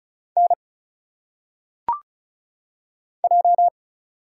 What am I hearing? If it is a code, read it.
Morse code, heard as a keyed tone, about 34 wpm, 700 Hz. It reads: N J
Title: Single Letters - Phonetic Pronunciation 35wpm